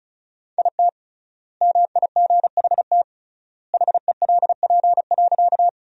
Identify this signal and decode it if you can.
Morse code, heard as a keyed tone, about 36 wpm, 700 Hz. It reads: IT MIGHT HELP.
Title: 3x Word-Spacing - Sentences from Top 300 Words - 35wpm